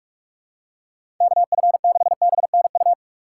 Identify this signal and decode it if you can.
Morse code, heard as a keyed tone, about 41 wpm, 700 Hz. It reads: KF6BNU
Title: USA Call Signs - 6 Characters - 4x Character Spacing - 45wpm